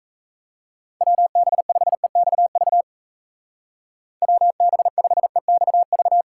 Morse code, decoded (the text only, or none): WB5EXV WB5EXV